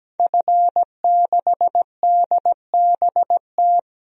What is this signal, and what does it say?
Morse code, heard as a keyed tone, about 17 wpm, 700 Hz. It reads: F6DBT
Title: DXCC-100 Call Signs - 5 Characters - 17wpm